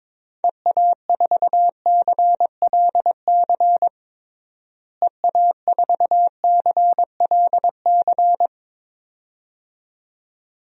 Morse code, heard as a keyed tone, about 22 wpm, 700 Hz. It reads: EA4CLC EA4CLC
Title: DXCC-100 Call Signs - 6 Characters - Repeated 2x - 22wpm